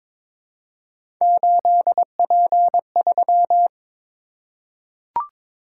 Morse code, heard as a keyed tone, about 22 wpm, 700 Hz. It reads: 8P3
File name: practice / 22wpm / Three Letter-Number Combinations - 22wpm